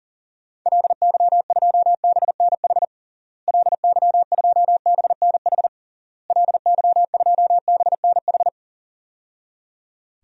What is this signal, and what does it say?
Morse code, heard as a keyed tone, about 40 wpm, 700 Hz. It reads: LY2BNH LY2BNH LY2BNH
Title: DXCC-100 Call Signs - 6 Characters - Repeated 3x - 40wpm